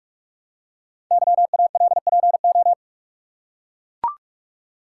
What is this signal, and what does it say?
Morse code, heard as a keyed tone, about 44 wpm, 700 Hz. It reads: YALPO
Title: Code Groups - 5 Letters - 45wpm